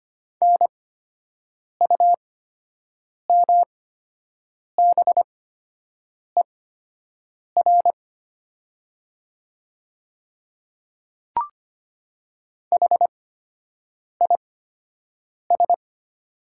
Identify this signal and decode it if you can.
Morse code, heard as a keyed tone, about 25 wpm, spaced as for 7 wpm, 700 Hz. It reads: NUMBER HIS
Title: 8x Character Spacing - Top 100 Words - 25wpm